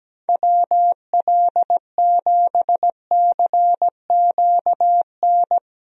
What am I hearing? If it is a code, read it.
Morse code, heard as a keyed tone, about 17 wpm, 700 Hz. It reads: WL7CQN
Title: USA Call Signs - 6 Characters - 17wpm